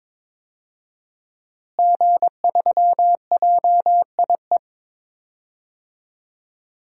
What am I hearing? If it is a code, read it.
Morse code, heard as a keyed tone, about 22 wpm, 700 Hz. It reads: G3JIE
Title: DXCC-100 Call Signs - 5 Characters - Repeated 3x - 22wpm